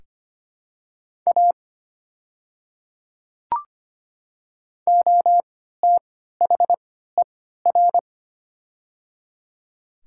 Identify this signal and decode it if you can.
Morse code, heard as a keyed tone, about 25 wpm, spaced as for 14 wpm, 700 Hz. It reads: A OTHER
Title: 3x Character Spacing - Top 100 Words - 25wpm